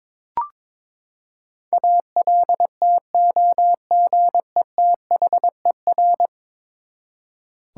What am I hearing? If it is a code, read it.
Morse code, heard as a keyed tone, about 22 wpm, 700 Hz. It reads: ALTOGETHER
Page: morse-code-ninja-practice-22wpm